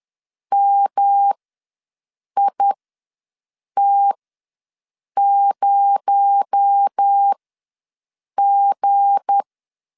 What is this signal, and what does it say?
Morse code, heard as a keyed tone, about 10 wpm, spaced as for 6 wpm, 800 Hz. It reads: MIT0G
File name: LEZIONI / 7 GF9Z / 7 G ws 10wpm 3843